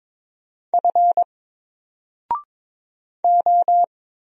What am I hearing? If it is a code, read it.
Morse code, heard as a keyed tone, about 22 wpm, 700 Hz. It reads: F O
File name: Single-Letter-Number-22wpm